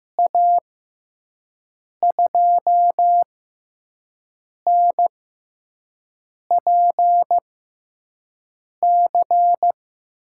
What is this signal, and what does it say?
Morse code, heard as a keyed tone, about 15 wpm, spaced as for 5 wpm, 700 Hz. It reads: A2NPC
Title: USA Call Signs - 6 Characters - 6x Character Spacing - 15wpm